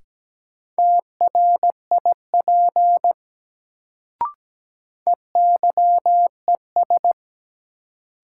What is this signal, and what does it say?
Morse code, heard as a keyed tone, about 17 wpm, 700 Hz. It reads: TRIP EYES
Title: Sentences from Top 1000 Words - Review Part 4 - New Word Drill - 17wpm